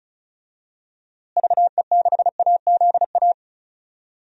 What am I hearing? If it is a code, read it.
Morse code, heard as a keyed tone, about 34 wpm, 700 Hz. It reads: VE6AZA
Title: DXCC-100 Call Signs - 6 Characters - Repeated 3x - 35wpm